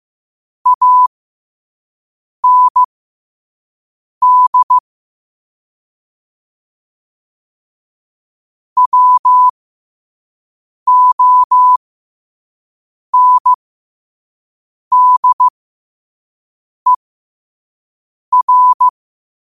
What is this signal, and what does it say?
Morse code, heard as a keyed tone, about 15 wpm, spaced as for 5 wpm, 1000 Hz. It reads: AND WONDER
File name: qotdfiles / 5wpm